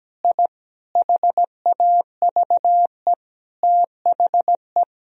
Morse code, heard as a keyed tone, about 17 wpm, 700 Hz. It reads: I HAVE THE
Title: Sentences from Top 1000 Words - Review Part 2 - 17wpm